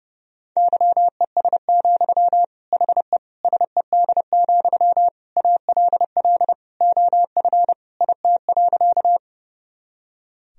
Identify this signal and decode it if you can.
Morse code, heard as a keyed tone, about 30 wpm, 700 Hz. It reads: YES, HE SED, ALL OF IT.
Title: Sentences from Top 500 Words and Abbreviations 30wpm